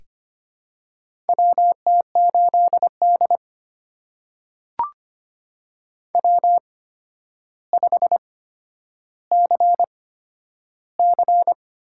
Morse code, heard as a keyed tone, about 25 wpm, spaced as for 11 wpm, 700 Hz. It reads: WT8D W5CC
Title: USA Call Signs - 4 Characters - 8x Character Spacing - 25wpm